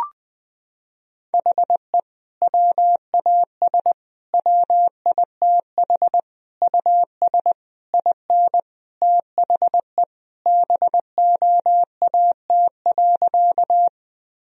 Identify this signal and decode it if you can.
Morse code, heard as a keyed tone, about 20 wpm, 700 Hz. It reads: HE WAS WITH US IN THE BOAT.
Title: Sentences from Top 1000 Words - Review Part 3 - 20wpm